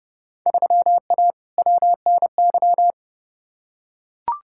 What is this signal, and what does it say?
Morse code, heard as a keyed tone, about 30 wpm, 700 Hz. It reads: 3A WNY
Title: Field Day Exchanges - 30wpm